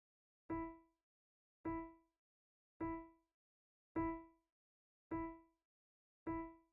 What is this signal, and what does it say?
There is no code.